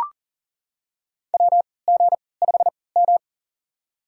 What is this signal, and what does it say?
Morse code, heard as a keyed tone, about 40 wpm, spaced as for 23 wpm, 700 Hz. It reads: WG5M
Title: USA Call Signs - 4 Characters - 3x Character Spacing - 40wpm